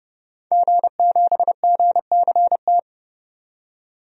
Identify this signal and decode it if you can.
Morse code, heard as a keyed tone, about 31 wpm, 700 Hz. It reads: G7GCT
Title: DXCC-100 Call Signs - 5 Characters - 30wpm